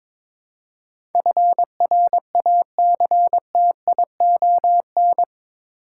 Morse code, heard as a keyed tone, about 22 wpm, 700 Hz. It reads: FRACTION
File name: practice / 22wpm / Eight Letter Words - 22wpm